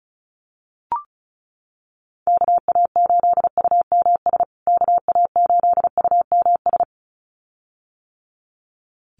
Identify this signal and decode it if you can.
Morse code, heard as a keyed tone, about 34 wpm, 700 Hz. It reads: KA8UMS KA8UMS
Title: USA Call Signs Repeated 2x 35wpm